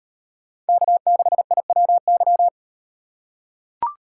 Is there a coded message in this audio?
Yes: Morse code from 0.7 to 2.5 s, heard as a keyed tone, about 38 wpm, 700 Hz. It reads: K6IWY